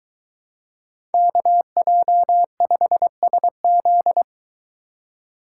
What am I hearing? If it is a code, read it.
Morse code, heard as a keyed tone, about 22 wpm, 700 Hz. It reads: KJ5SZ